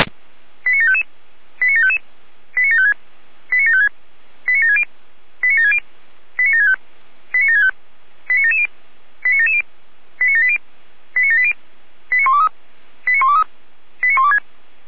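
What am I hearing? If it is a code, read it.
Five-tone selective call (ZVEI), heard as quick runs of five tones, tones of about 70 ms. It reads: 89755 89755 89756 89756 89760 89760 89765 89765 89770 89770 89780 89780 89123 89123 89126
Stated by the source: ZVEI five-tone selective call received by SDR.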